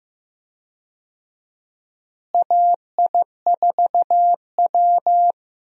Morse code, heard as a keyed tone, about 15 wpm, 700 Hz. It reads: AI4W
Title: USA Call Signs - 4 Characters - Repeated 2x - 15wpm